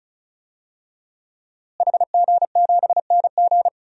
Morse code, heard as a keyed tone, about 36 wpm, 700 Hz. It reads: HG7NG
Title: DXCC-100 Call Signs - 5 Characters - Repeated 2x - 35wpm